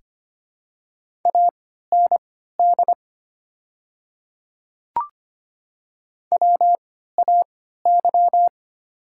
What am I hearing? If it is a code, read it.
Morse code, heard as a keyed tone, about 25 wpm, spaced as for 14 wpm, 700 Hz. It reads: AND WAY
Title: 3x Character Spacing - Top 100 Words - 25wpm